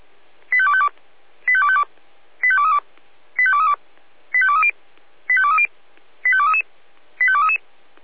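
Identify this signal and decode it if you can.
Five-tone selective call (ZVEI), heard as quick runs of five tones, tones of about 70 ms. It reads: 85252 85252 85232 85232 85230 85230 85233 85233
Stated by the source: ZVEI five-tone selective call received with an SDR.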